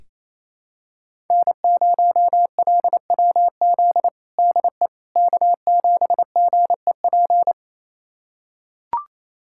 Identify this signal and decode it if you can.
Morse code, heard as a keyed tone, about 28 wpm, 700 Hz. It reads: N0LWZ DE K7GEP
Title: USA Callsign DE Callsign - 5 Characters - 28wpm